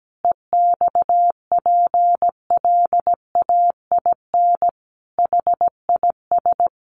Morse code, heard as a keyed tone, about 17 wpm, 700 Hz. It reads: EXPLAIN HIS
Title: Sentences from Top 1300 Words 17wpm